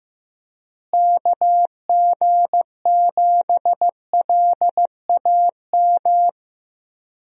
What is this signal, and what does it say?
Morse code, heard as a keyed tone, about 15 wpm, 700 Hz. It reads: KG7LAM
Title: USA Call Signs - 6 Characters - Repeated 3x - 15wpm